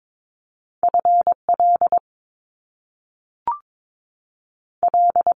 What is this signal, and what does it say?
Morse code, heard as a keyed tone, about 22 wpm, 700 Hz. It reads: F L L
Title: US States 22wpm